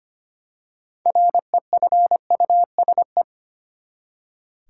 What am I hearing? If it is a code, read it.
Morse code, heard as a keyed tone, about 25 wpm, 700 Hz. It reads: REFUSE
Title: Six Letter Words - 25wpm